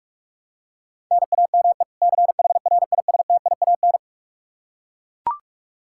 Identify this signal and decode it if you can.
Morse code, heard as a keyed tone, about 41 wpm, 700 Hz. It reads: NAME CHRISTIAN
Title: K1USN SST QSO Practice 45wpm